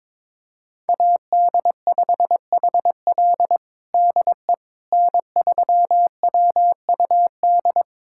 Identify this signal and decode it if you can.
Morse code, heard as a keyed tone, about 22 wpm, 700 Hz. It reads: AD5HL DE N3WUD
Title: USA Callsign DE Callsign - 5 Characters - 22wpm